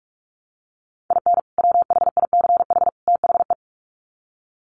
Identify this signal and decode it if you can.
Morse code, heard as a keyed tone, about 44 wpm, 700 Hz. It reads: IN WHICH THE
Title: Sets of 3 Words 45wpm